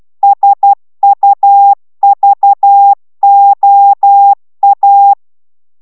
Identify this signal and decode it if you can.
Morse code, heard as a keyed tone, about 12 wpm, 800 Hz. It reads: SUVOA